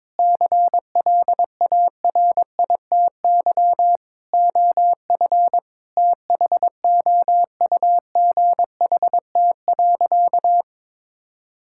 Morse code, heard as a keyed tone, about 22 wpm, 700 Hz. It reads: CLARITY OF THOUGHT.
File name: Difficult Sentences with Hints - 22wpm